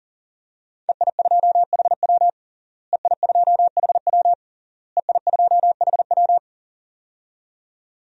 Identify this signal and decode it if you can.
Morse code, heard as a keyed tone, about 40 wpm, 700 Hz. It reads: EI2HW EI2HW EI2HW
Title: DXCC-100 Call Signs - 5 Characters - Repeated 3x - 40wpm